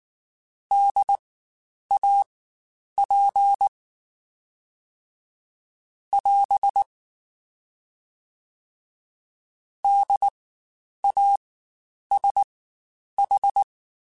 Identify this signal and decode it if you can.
Morse code, heard as a keyed tone, about 19 wpm, spaced as for 9 wpm, 800 Hz. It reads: DAP <AS> DASH
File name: Spaced Alphas / Spaced Alphas mp3s / Ds sp @18